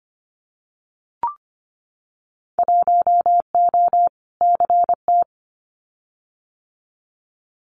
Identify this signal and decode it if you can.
Morse code, heard as a keyed tone, about 25 wpm, 700 Hz. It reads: 1O CT
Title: Winter Field Day Exchanges 25wpm